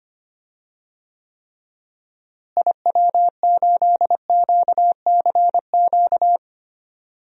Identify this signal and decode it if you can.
Morse code, heard as a keyed tone, about 25 wpm, 700 Hz. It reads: IW8QCQ
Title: DXCC-100 Call Signs - 6 Characters - Repeated 3x - 25wpm